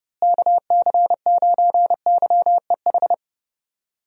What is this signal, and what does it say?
Morse code, heard as a keyed tone, about 31 wpm, 700 Hz. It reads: KC9YEH
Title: USA Call Signs - 6 Characters - Repeated 2x - 30wpm